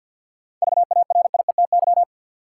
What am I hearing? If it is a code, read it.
Morse code, heard as a keyed tone, about 53 wpm, 700 Hz. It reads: VARIETY
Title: Seven Letter Words - 50wpm